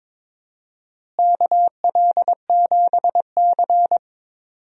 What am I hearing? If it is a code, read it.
Morse code, heard as a keyed tone, about 22 wpm, 700 Hz. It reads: KL7C
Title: DXCC-100 Call Signs - 4 Characters - Repeated 3x - 22wpm